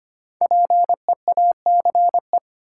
Morse code, heard as a keyed tone, about 25 wpm, 700 Hz. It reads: PEACE
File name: 5 Letter Words - First 3 Letters Given - 25wpm